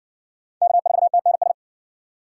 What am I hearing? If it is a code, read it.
Morse code, heard as a keyed tone, about 60 wpm, 700 Hz. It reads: K4TNS